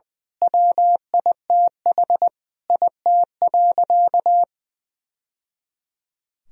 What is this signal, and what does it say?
Morse code, heard as a keyed tone, about 20 wpm, 700 Hz. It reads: WITH IT.
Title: Sentences from Top 1000 Words - Review Part 3 - 20wpm